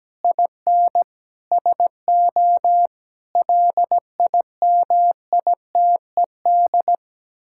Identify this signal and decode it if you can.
Morse code, heard as a keyed tone, about 17 wpm, 700 Hz. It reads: IN SO LIMITED
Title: Sentences from Top 1000 Words - Review Part 6 - 17wpm